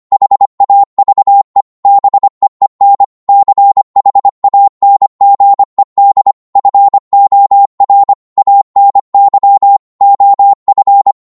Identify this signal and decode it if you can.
Morse code, heard as a keyed tone, about 25 wpm, 800 Hz. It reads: HAVE BEEN CHANGED FOR ANY OF